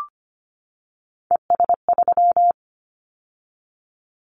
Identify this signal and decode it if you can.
Morse code, heard as a keyed tone, about 25 wpm, 700 Hz. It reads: ES3
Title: Callsign Prefixes for Top 100 DXCC - 25wpm